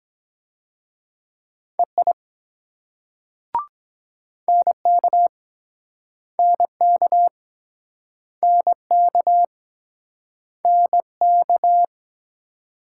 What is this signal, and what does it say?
Morse code, heard as a keyed tone, about 23 wpm, 700 Hz. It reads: EI NK NK NK NK